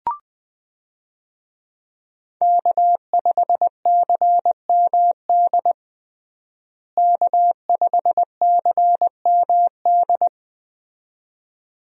Morse code, heard as a keyed tone, about 20 wpm, 700 Hz. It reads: K5CMD K5CMD
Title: USA Call Signs - 5 Characters - Repeated 2x - 20wpm